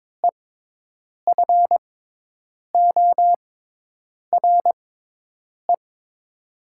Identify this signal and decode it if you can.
Morse code, heard as a keyed tone, about 22 wpm, spaced as for 8 wpm, 700 Hz. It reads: EFORE
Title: 6x Character Spacing - Top 300 Words - 22wpm